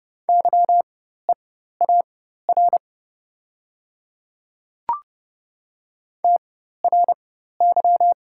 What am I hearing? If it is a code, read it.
Morse code, heard as a keyed tone, about 30 wpm, spaced as for 14 wpm, 700 Hz. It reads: YEAR TRY
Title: 4x Character Spacing - Top 500 Words - 30wpm